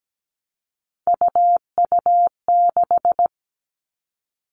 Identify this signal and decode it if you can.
Morse code, heard as a keyed tone, about 17 wpm, 700 Hz. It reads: UU6
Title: Three Challenging Letter-Number Combinations 17wpm